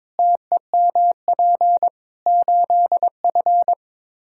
Morse code, heard as a keyed tone, about 21 wpm, 700 Hz. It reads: TEMP 8F
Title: Practice QSO - Exchange 2 - Info - 22wpm